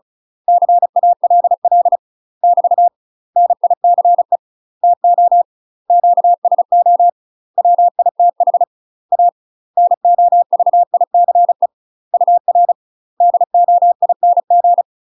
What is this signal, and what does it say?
Morse code, heard as a keyed tone, about 34 wpm, 700 Hz. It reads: CALL <BT> NICE TO QSO WITH A NOVICE UR DOING
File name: UK MASTER QSOs TRAINING / 35WPM / UK MASTER QSOs 0042